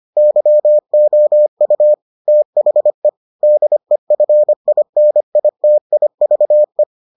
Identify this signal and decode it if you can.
Morse code, heard as a keyed tone, about 25 wpm, 600 Hz. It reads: YOU THE DEFINITIVE